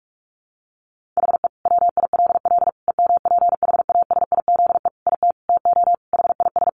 Morse code, heard as a keyed tone, about 44 wpm, 700 Hz. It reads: HE WILL EMPHASIZE IT TO HIS
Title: Sentences from Top 1500 Words 45wpm